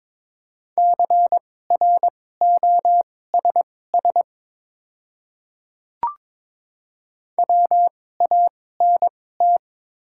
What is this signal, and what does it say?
Morse code, heard as a keyed tone, about 22 wpm, spaced as for 16 wpm, 700 Hz. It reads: CROSS WANT
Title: 2x Character Spacing - Top 300 Words - 22wpm